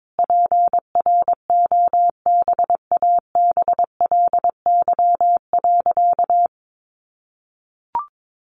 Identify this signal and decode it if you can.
Morse code, heard as a keyed tone, about 22 wpm, 700 Hz. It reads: PROBABLY.
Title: Sentences from Top 700 Words 22wpm